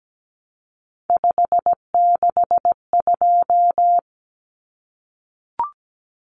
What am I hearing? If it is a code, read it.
Morse code, heard as a keyed tone, about 17 wpm, 700 Hz. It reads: 562
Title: Common Cut Numbers - 3-Digit 17wpm